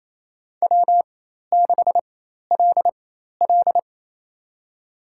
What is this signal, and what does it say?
Morse code, heard as a keyed tone, about 28 wpm, spaced as for 13 wpm, 700 Hz. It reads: W6LL
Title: USA Call Signs - 4 Characters - 4x Character Spacing - 28wpm